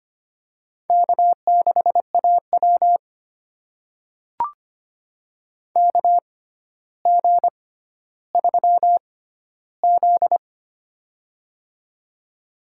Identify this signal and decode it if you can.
Morse code, heard as a keyed tone, about 25 wpm, spaced as for 13 wpm, 700 Hz. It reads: K6AW KG3Z
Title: USA Call Signs - 6x Character Spacing - 25wpm